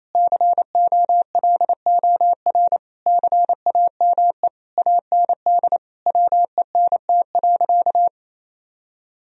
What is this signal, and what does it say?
Morse code, heard as a keyed tone, about 28 wpm, 700 Hz. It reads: COLOR CAME AND WENT.
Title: Sentences from Top 1000 Words - Review Part 2 - 28wpm